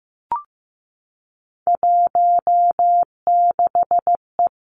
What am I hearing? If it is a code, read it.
Morse code, heard as a keyed tone, about 15 wpm, 700 Hz. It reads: 16E
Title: Three Challenging Letter-Number Combinations 15wpm